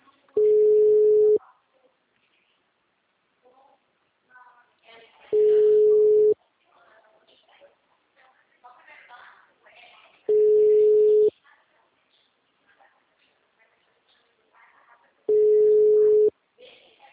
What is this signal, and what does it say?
Ringback tone (Europe, 425 Hz), with speech.
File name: out- -2010- - - .29005